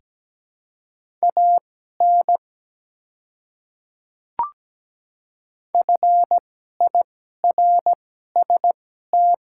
Morse code, heard as a keyed tone, about 17 wpm, spaced as for 12 wpm, 700 Hz. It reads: AN FIRST
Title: 2x Character Spacing - Top 200 Words - 17wpm